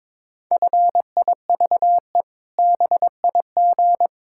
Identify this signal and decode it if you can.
Morse code, heard as a keyed tone, about 22 wpm, 700 Hz. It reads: FIVE BIG
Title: Pangram Challenge v2 - 22wpm